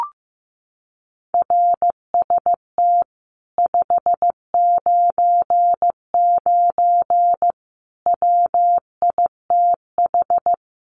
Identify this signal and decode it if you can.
Morse code, heard as a keyed tone, about 15 wpm, 700 Hz. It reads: RST 599 WITH
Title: Common QSO Phrases Encore 15wpm